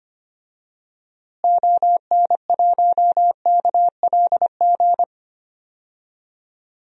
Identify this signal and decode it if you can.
Morse code, heard as a keyed tone, about 25 wpm, 700 Hz. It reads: ON1KLG
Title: DXCC-100 Call Signs - 6 Characters - Repeated 3x - 25wpm